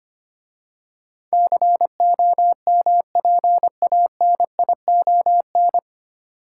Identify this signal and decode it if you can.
Morse code, heard as a keyed tone, about 25 wpm, 690 Hz. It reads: COMPANION
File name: practice / 25wpm / 9 Letter Words - First 3 Letters Given - 25wpm